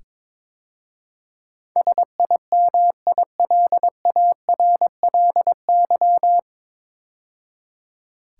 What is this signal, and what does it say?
Morse code, heard as a keyed tone, about 22 wpm, 700 Hz. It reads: SIMILARLY
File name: 9 Letter Words - First 3 Letters Given - 22wpm